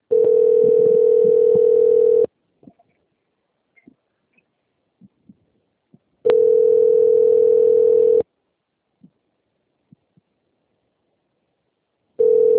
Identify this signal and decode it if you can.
Ringback tone (North America), 440 + 480 Hz.
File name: out- - - - - .85830